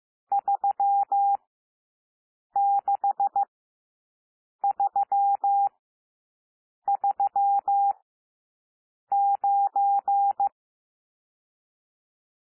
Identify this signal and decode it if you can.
Morse code, heard as a keyed tone, about 15 wpm, spaced as for 6 wpm, 800 Hz. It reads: 36339
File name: cwmp3 / N1 SOUND NUMBERS 15 wpm